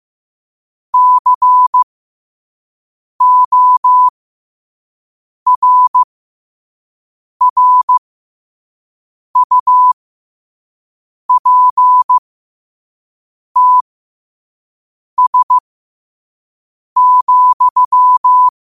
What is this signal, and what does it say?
Morse code, heard as a keyed tone, about 15 wpm, spaced as for 5 wpm, 1000 Hz. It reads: CORRUPTS,
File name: 5wpm